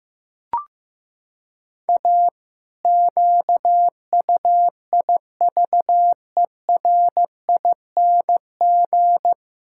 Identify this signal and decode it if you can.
Morse code, heard as a keyed tone, about 15 wpm, 700 Hz. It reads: A QUIVERING
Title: Pangram Challenge v2 - 15wpm